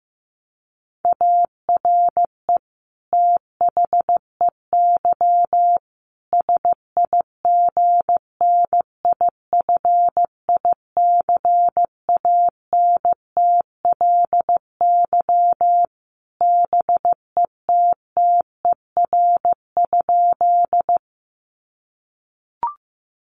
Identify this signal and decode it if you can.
Morse code, heard as a keyed tone, about 14 wpm, 700 Hz. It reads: ARE THEY SIGNIFICANTLY BETTER?
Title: Sentences from Top 1500 Words 15wpm